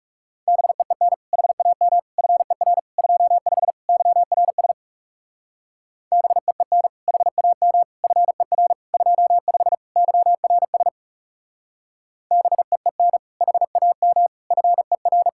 Morse code, heard as a keyed tone, about 36 wpm, 700 Hz. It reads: BEEN HAM FER 25 YRS BEEN HAM FER 25 YRS BEEN HAM FER
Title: Speed-Racing Common QSO Phrases Encore - 30wpm